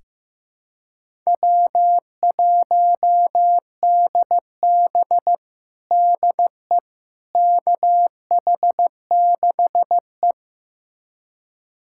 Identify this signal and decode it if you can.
Morse code, heard as a keyed tone, about 15 wpm, 700 Hz. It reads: W1DB DE KH6E
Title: USA Callsign DE Callsign - 4 Characters - 15wpm